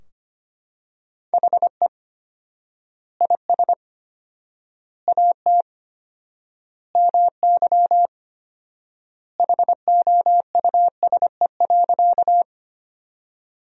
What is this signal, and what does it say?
Morse code, heard as a keyed tone, about 25 wpm, 700 Hz. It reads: HE IS AT MY HOUSE.
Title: 4x Word-Spacing - Sentences from Top 200 Words - 25wpm